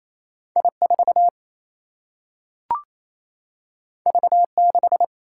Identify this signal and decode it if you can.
Morse code, heard as a keyed tone, about 28 wpm, 700 Hz. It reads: I 4 V 6
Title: Single-Letter Single-Number 28wpm